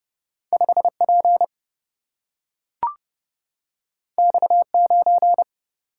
Morse code, heard as a keyed tone, about 29 wpm, 700 Hz. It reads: 5P X9
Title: Single-Letter Single-Number 30wpm